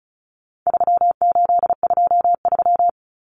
Morse code, heard as a keyed tone, about 34 wpm, 700 Hz. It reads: 3823
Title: Common Cut Numbers - 4-Digit 35wpm